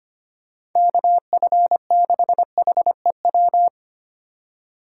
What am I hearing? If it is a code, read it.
Morse code, heard as a keyed tone, about 25 wpm, 700 Hz. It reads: KF6HEW
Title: USA Call Signs - 4x Character Spacing - 25wpm